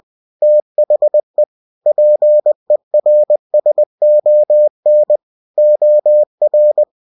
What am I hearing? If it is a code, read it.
Morse code, heard as a keyed tone, about 20 wpm, 600 Hz. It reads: THE PERSON OR
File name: cw-books / winnie-the-pooh / winnie-20wpm-page-0119